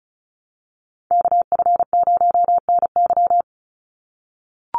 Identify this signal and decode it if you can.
Morse code, heard as a keyed tone, about 34 wpm, 700 Hz. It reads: KF0NY
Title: USA Call Signs 35wpm